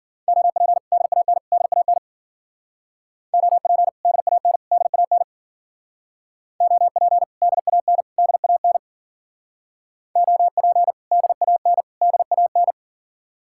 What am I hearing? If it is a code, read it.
Morse code, heard as a keyed tone, about 51 wpm, 700 Hz. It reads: OP DAN DAN OP DAN DAN OP DAN DAN OP DAN DAN